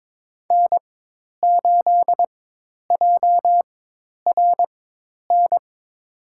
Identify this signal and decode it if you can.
Morse code, heard as a keyed tone, about 22 wpm, spaced as for 10 wpm, 700 Hz. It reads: N8JRN
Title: USA Call Signs - 5 Characters - 4x Character Spacing - 22wpm